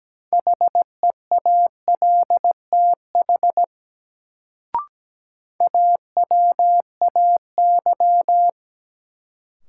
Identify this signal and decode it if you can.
Morse code, heard as a keyed tone, about 17 wpm, 700 Hz. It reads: HEALTH AWAY
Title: Sentences from Top 600 Words - New Word Drill - 17wpm